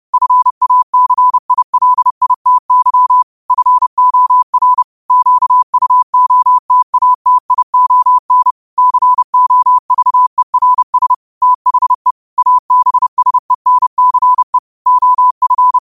Morse code, heard as a keyed tone, about 30 wpm, 1000 Hz. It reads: FACILITY FOR QUOTATION COVERS THE ABSENCE OF